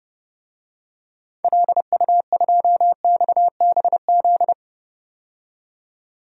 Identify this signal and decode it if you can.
Morse code, heard as a keyed tone, about 32 wpm, 700 Hz. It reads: LU2XBZ
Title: DXCC-100 Call Signs - 6 Characters - 30wpm